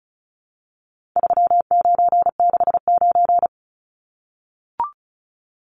Morse code, heard as a keyed tone, about 34 wpm, 700 Hz. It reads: 3969
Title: Numbers - 4-Digit 35wpm